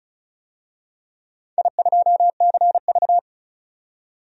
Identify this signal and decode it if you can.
Morse code, heard as a keyed tone, about 34 wpm, 700 Hz. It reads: I2CV
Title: DXCC-100 Call Signs - 4 Characters - 35wpm